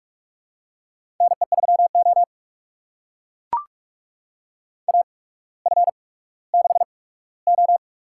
Morse code, heard as a keyed tone, about 44 wpm, spaced as for 19 wpm, 700 Hz. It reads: NE3O AF6O
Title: USA Call Signs - 8x Character Spacing - 45wpm